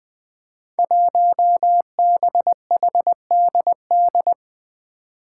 Morse code, heard as a keyed tone, about 20 wpm, 700 Hz. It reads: 1BHDD
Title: Five Challenging Letter-Number Combinations 20wpm